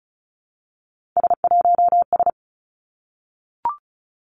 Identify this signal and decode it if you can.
Morse code, heard as a keyed tone, about 36 wpm, 700 Hz. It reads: S1S
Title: Three Challenging Letter-Number Combinations 35wpm